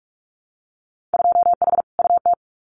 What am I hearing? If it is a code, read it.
Morse code, heard as a keyed tone, about 46 wpm, 700 Hz. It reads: 2H UT
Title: Winter Field Day Exchanges Encore 45wpm